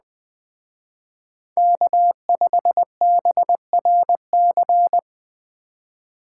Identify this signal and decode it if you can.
Morse code, heard as a keyed tone, about 20 wpm, 700 Hz. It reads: K5BRC
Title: Speed Racing - USA 1x3 Call Signs 17wpm